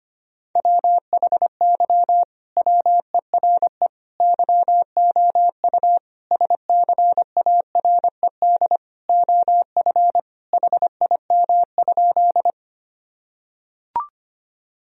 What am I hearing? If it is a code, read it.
Morse code, heard as a keyed tone, about 25 wpm, 700 Hz. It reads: WHY WERE YOU SCARED OF HIM?
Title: Sentences from Top 1000 Words - Review Part 4 - 25wpm